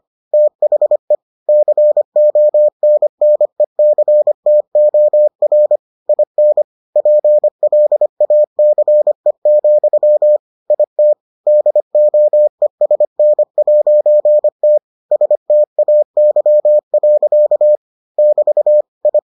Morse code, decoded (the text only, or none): THE CONNECTOR IN PLACE, IT DOESN'T STAY. <BT> I